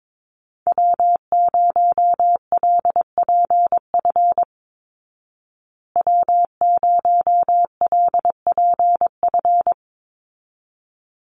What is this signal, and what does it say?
Morse code, heard as a keyed tone, about 22 wpm, 700 Hz. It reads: W0LPF W0LPF
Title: Call Signs Repeated 3x - 22wpm